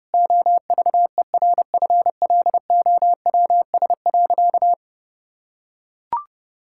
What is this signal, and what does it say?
Morse code, heard as a keyed tone, about 30 wpm, 700 Hz. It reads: OVERFLOWS.
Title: Pangram Challenge v2 - 30wpm